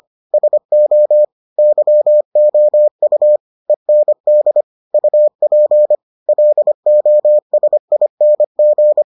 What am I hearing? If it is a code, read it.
Morse code, heard as a keyed tone, about 25 wpm, 600 Hz. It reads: SO YOU END UP LOSING